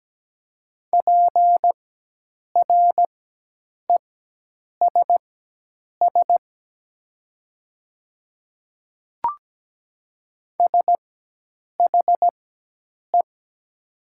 Morse code, heard as a keyed tone, about 17 wpm, spaced as for 8 wpm, 700 Hz. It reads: PRESS SHE
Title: 4x Character Spacing - Top 500 Words - 17wpm